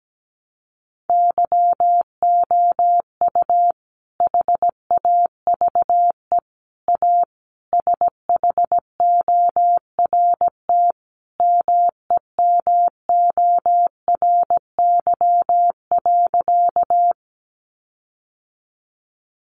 Morse code, heard as a keyed tone, about 17 wpm, 700 Hz. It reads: YOU HAVE A SHORT MEMORY.
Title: Sentences from Top 1300 Words 17wpm